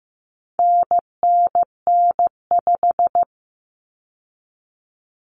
Morse code, heard as a keyed tone, about 15 wpm, 700 Hz. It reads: NNN5
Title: Common Cut Numbers - 4-Digit 15wpm